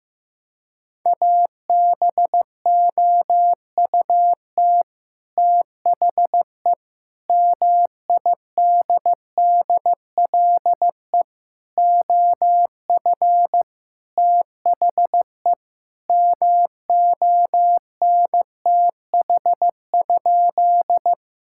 Morse code, heard as a keyed tone, about 15 wpm, 700 Hz. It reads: ABOUT THE MIDDLE OF THE MONTH?
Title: Sentences from Top 1000 Words - Review Part 6 - 15wpm